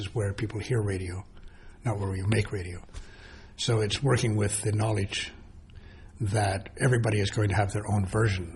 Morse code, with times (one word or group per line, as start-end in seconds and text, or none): none